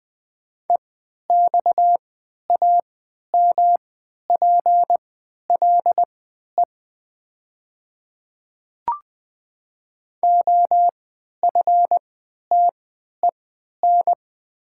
Morse code, heard as a keyed tone, about 20 wpm, spaced as for 11 wpm, 700 Hz. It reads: EXAMPLE OFTEN
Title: 3x Character Spacing - Top 300 Words - 20wpm